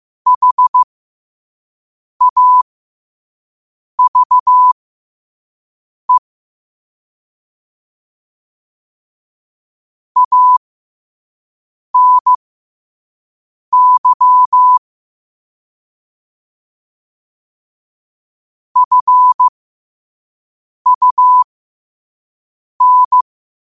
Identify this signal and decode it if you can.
Morse code, heard as a keyed tone, about 15 wpm, spaced as for 5 wpm, 1000 Hz. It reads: HAVE ANY FUN